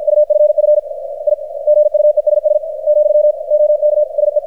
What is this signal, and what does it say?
Morse code, heard as a keyed tone, about 53 wpm, 600 Hz, with radio noise. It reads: UFF I GUESS , OK SE
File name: archived / CRSnet